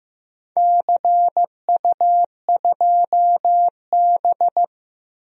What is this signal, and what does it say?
Morse code, heard as a keyed tone, about 15 wpm, 700 Hz. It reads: CU2B
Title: DXCC-100 Call Signs - 4 Characters - Repeated 3x - 15wpm